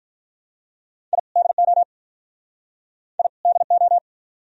Morse code, heard as a keyed tone, about 53 wpm, 700 Hz. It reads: I DO I DO